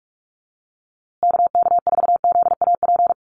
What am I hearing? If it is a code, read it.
Morse code, heard as a keyed tone, about 44 wpm, 700 Hz. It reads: KK4ZAP
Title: USA Call Signs Repeated 2x 45wpm